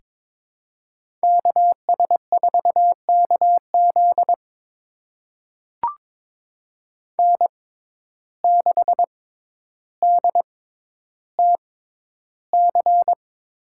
Morse code, heard as a keyed tone, about 21 wpm, spaced as for 11 wpm, 700 Hz. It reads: KS4KZ N6DTC
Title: USA Call Signs - 6x Character Spacing - 22wpm